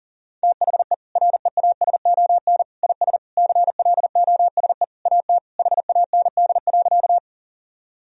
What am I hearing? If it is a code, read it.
Morse code, heard as a keyed tone, about 40 wpm, 700 Hz. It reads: THE REASON IS CLOSE AT HAND.
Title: Sentences from Top 1000 Words - Review Part 1 - 40wpm